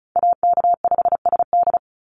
Morse code, heard as a keyed tone, about 36 wpm, 700 Hz. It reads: AK5SD
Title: NA SOTA Chaser Callsigns 35wpm